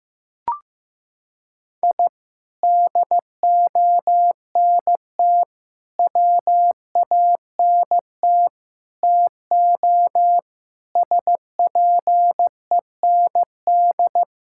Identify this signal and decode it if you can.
Morse code, heard as a keyed tone, about 15 wpm, 700 Hz. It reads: I DONT WANT TO SPEND
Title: Sentences from Top 1000 Words - Review Part 5 - 15wpm